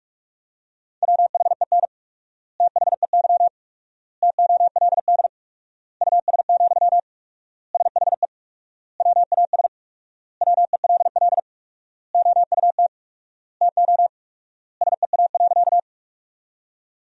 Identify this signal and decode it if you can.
Morse code, heard as a keyed tone, about 46 wpm, 700 Hz. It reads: WHEN THEY TOLD US, SHE WAS WELL OUT TO SEA.
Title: 4x Word-Spacing - Sentences from Top 400 Words - 45wpm